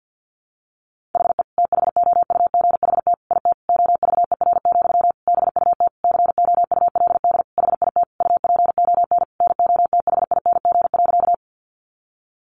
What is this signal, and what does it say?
Morse code, heard as a keyed tone, about 50 wpm, 700 Hz. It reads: HE THOUGHT IT OVER, BUT COULD HIT UPON NOTHING.
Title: Sentences from Top 1200 Words 50wpm